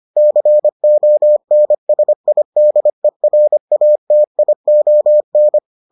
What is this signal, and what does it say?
Morse code, heard as a keyed tone, about 25 wpm, 600 Hz. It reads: CONSIDERATION